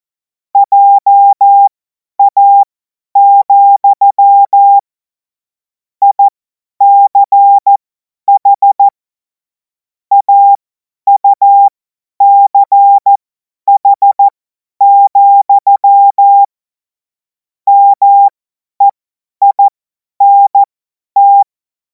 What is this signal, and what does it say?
Morse code, heard as a keyed tone, about 14 wpm, spaced as for 10 wpm, 800 Hz. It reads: JA, ICH AUCH, MEINT
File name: Morsekurs / Witze / Witz 17. Februar 2026 50 F70